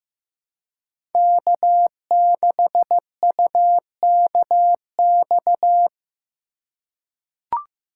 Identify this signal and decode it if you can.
Morse code, heard as a keyed tone, about 15 wpm, 700 Hz. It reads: K6UKX
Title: USA Call Signs - 2x Character Spacing - 15wpm